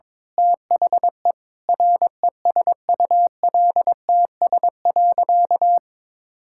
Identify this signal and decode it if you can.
Morse code, heard as a keyed tone, about 21 wpm, 700 Hz. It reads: THE RESULTS.
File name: Sentences with 7-Word Phrases - 22wpm